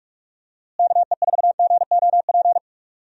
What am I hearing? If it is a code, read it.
Morse code, heard as a keyed tone, about 46 wpm, 700 Hz. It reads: KE4GOP
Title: USA Call Signs - 6 Characters - Repeated 3x - 45wpm